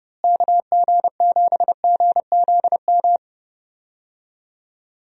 Morse code, heard as a keyed tone, about 30 wpm, 700 Hz. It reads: KG7GZM
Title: USA Call Signs - 6 Characters - Repeated 2x - 30wpm